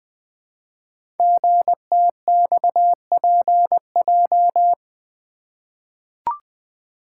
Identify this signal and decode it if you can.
Morse code, heard as a keyed tone, about 20 wpm, 700 Hz. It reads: GTXPJ
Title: Code Groups - 5 Letters - 20wpm